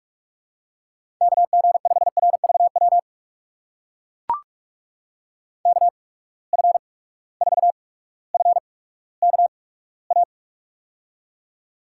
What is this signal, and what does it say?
Morse code, heard as a keyed tone, about 44 wpm, spaced as for 19 wpm, 700 Hz. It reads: KG5RVW KF4FKA
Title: USA Call Signs - 6 Characters - 8x Character Spacing - 45wpm